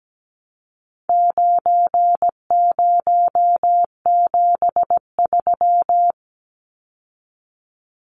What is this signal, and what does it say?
Morse code, heard as a keyed tone, about 17 wpm, 700 Hz. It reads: 9073
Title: Numbers - 4-Digit 17wpm